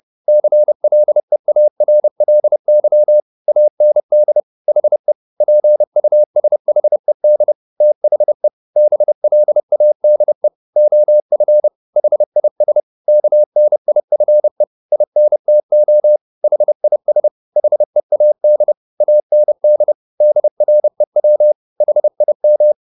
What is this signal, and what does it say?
Morse code, heard as a keyed tone, about 30 wpm, 600 Hz. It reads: CLEARLY AND HE PUSHED THE BLADE OF HIS KNIFE INTO HIS HEAD AND DREW HIM